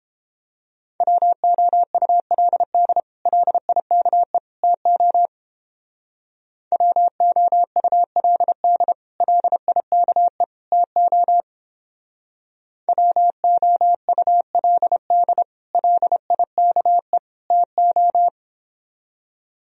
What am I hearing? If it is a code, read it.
Morse code, heard as a keyed tone, about 29 wpm, 700 Hz. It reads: WOULD LIKE TO WOULD LIKE TO WOULD LIKE TO